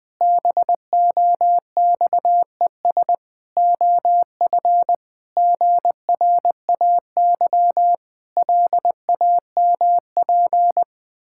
Morse code, heard as a keyed tone, about 20 wpm, 700 Hz. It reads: BOXES OF GRAY LAMP
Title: Pangram Challenge v2 - 20wpm